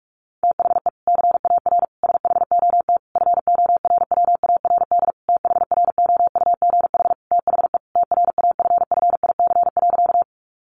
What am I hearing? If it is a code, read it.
Morse code, heard as a keyed tone, about 44 wpm, 700 Hz. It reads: THE CAR SHOT FORWARD THROUGH THE TRAFFIC.
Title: Sentences from Top 1300 Words 45wpm